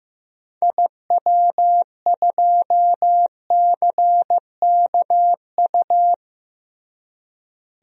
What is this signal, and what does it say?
Morse code, heard as a keyed tone, about 15 wpm, 700 Hz. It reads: IW2CKU
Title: DXCC-100 Call Signs - 6 Characters - Repeated 2x - 15wpm